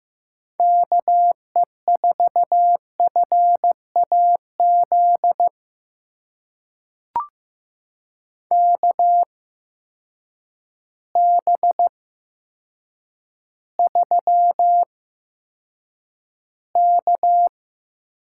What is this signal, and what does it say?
Morse code, heard as a keyed tone, about 15 wpm, spaced as for 8 wpm, 700 Hz. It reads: KE4FAZ KB3K
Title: USA Call Signs - 6 Characters - 8x Character Spacing - 15wpm